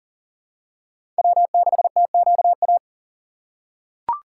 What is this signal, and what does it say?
Morse code, heard as a keyed tone, about 40 wpm, 700 Hz. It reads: W6TQA
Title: USA Call Signs - 5 Characters - 2x Character Spacing - 40wpm